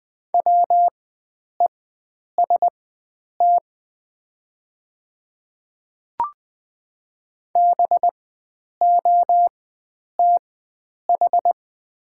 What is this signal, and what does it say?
Morse code, heard as a keyed tone, about 20 wpm, spaced as for 9 wpm, 700 Hz. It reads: WEST BOTH
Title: 4x Character Spacing - Top 500 Words - 20wpm